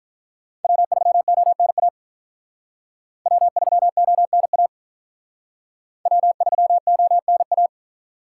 Morse code, heard as a keyed tone, about 42 wpm, 700 Hz. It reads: W3ONA W3ONA W3ONA
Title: Speed Racing - USA 1x3 Call Signs 35wpm